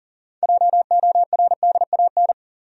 Morse code, heard as a keyed tone, about 40 wpm, 700 Hz. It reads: JORDAN
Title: USA Common Names - 40wpm